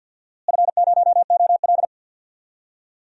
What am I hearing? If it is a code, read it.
Morse code, heard as a keyed tone, about 51 wpm, 700 Hz. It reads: F0OL
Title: DXCC-100 Call Signs - 4 Characters - Repeated 3x - 50wpm